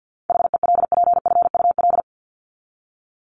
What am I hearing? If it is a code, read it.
Morse code, heard as a keyed tone, about 50 wpm, 700 Hz. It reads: HELPFUL
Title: Top 751-1000 Words 50wpm